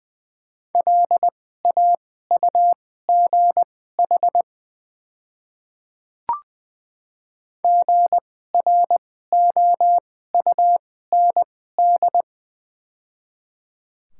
Morse code, heard as a keyed tone, about 20 wpm, spaced as for 14 wpm, 700 Hz. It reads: LAUGH GROUND
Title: 2x Character Spacing - Top 500 Words - 20wpm